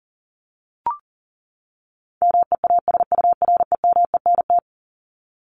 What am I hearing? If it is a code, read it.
Morse code, heard as a keyed tone, about 40 wpm, 700 Hz. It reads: MEASUREMENT
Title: Top 1751-2000 Words 40wpm